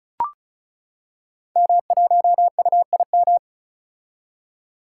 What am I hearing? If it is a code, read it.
Morse code, heard as a keyed tone, about 36 wpm, 700 Hz. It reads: M1UIM
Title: Code Groups - 5 Letter:Numbers - 35wpm